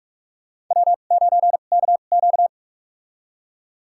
Morse code, heard as a keyed tone, about 46 wpm, spaced as for 32 wpm, 700 Hz. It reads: W9KQ
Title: USA Call Signs - 2x Character Spacing - 45wpm